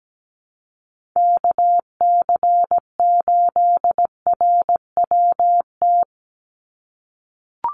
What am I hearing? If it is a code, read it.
Morse code, heard as a keyed tone, about 17 wpm, 700 Hz. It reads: KC8RWT
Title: USA Call Signs 17wpm